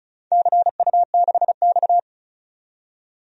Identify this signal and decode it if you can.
Morse code, heard as a keyed tone, about 36 wpm, 690 Hz. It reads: CU6X
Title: DXCC-100 Call Signs - 4 Characters - 35wpm